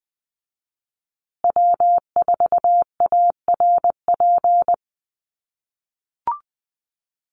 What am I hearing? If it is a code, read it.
Morse code, heard as a keyed tone, about 20 wpm, 700 Hz. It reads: W4ARP
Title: USA Call Signs Repeated 3x 20wpm